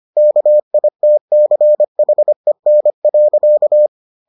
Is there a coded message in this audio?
Yes: Morse code from 0.2 to 3.9 s, heard as a keyed tone, about 25 wpm, 600 Hz. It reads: KITCHEN.